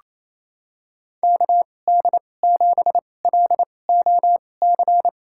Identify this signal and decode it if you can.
Morse code, heard as a keyed tone, about 28 wpm, spaced as for 20 wpm, 700 Hz. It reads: KD7LOC
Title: USA Call Signs - 6 Characters - 2x Character Spacing - 28wpm